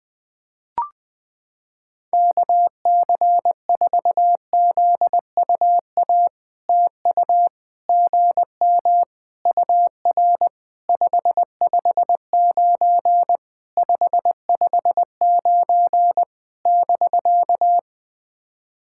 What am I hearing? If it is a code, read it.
Morse code, heard as a keyed tone, about 20 wpm, 700 Hz. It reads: KC4ZUA TU GM UR 559 559 <BK>
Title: POTA QSO Practice - 20wpm